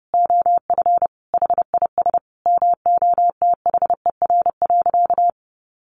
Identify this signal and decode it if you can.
Morse code, heard as a keyed tone, about 30 wpm, 700 Hz. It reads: OF HIS MOTHER.
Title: Sentences from Top 800 Words 30wpm